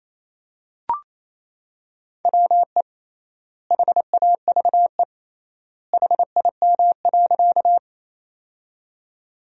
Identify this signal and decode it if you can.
Morse code, heard as a keyed tone, about 28 wpm, 700 Hz. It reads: WE HAVE HIM.
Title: 3x Word-Spacing - Sentences from Top 100 Words - 28wpm